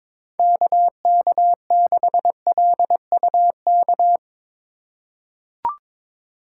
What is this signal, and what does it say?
Morse code, heard as a keyed tone, about 22 wpm, 700 Hz. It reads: KK6LUK
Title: USA Call Signs - 6 Characters - 2x Character Spacing - 22wpm